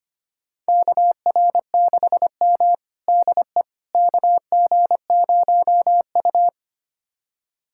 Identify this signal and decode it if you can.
Morse code, heard as a keyed tone, about 25 wpm, 700 Hz. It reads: KR6M DE KG0U
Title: USA Callsign DE Callsign - 4 Characters - 25wpm